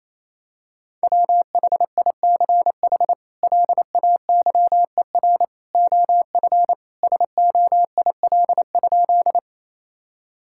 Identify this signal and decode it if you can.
Morse code, heard as a keyed tone, about 27 wpm, 700 Hz. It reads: WHICH LAYER OF SOIL?
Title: Sentences from Top 1600 Words 28wpm